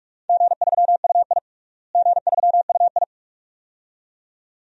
Morse code, heard as a keyed tone, about 42 wpm, 700 Hz. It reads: G3UI G3UI
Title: DXCC-100 Call Signs - 4 Characters - Repeated 2x - 45wpm